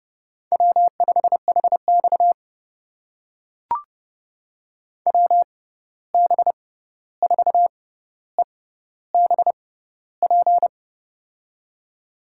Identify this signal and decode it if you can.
Morse code, heard as a keyed tone, about 30 wpm, spaced as for 14 wpm, 700 Hz. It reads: W5HX WB4EBP
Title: USA Call Signs - 6x Character Spacing - 30wpm